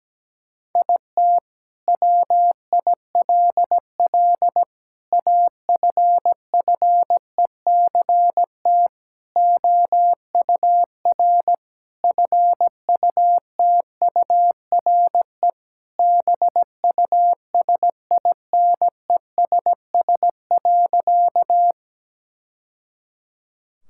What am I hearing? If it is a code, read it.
Morse code, heard as a keyed tone, about 17 wpm, 700 Hz. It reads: IT WILL AFFECT OUR FUTURE BUSINESS.
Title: Sentences from Top 1000 Words - Review Part 6 - 17wpm